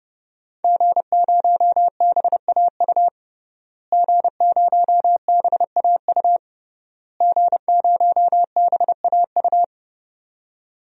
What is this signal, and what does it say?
Morse code, heard as a keyed tone, about 30 wpm, 700 Hz. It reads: G0BAU G0BAU G0BAU
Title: DXCC-100 Call Signs - 5 Characters - Repeated 3x - 30wpm